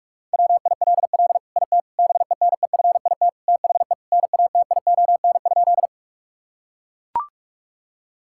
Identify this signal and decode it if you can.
Morse code, heard as a keyed tone, about 41 wpm, 700 Hz. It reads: WILL IT BENEFIT THE NATION?
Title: Sentences from Top 1000 Words - Review Part 6 - 45wpm